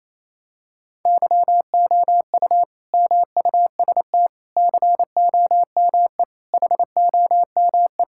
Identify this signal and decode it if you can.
Morse code, heard as a keyed tone, about 28 wpm, 700 Hz. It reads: YOU MUST COME HOME
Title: Sentences from Top 1000 Words - Review Part 5 - 28wpm